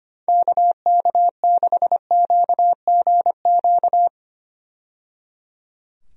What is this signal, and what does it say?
Morse code, heard as a keyed tone, about 25 wpm, 700 Hz. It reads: KK6QGQ
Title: USA Call Signs - 6 Characters - 25wpm